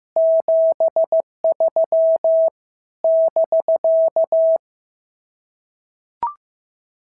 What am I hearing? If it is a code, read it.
Morse code, heard as a keyed tone, about 14 wpm, 650 Hz. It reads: 73 <BK>
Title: SOTA QSO Practice 15wpm